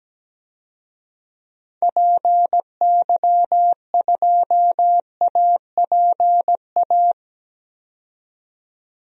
Morse code, heard as a keyed tone, about 17 wpm, 700 Hz. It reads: PY2APA
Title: DXCC-100 Call Signs - 6 Characters - 17wpm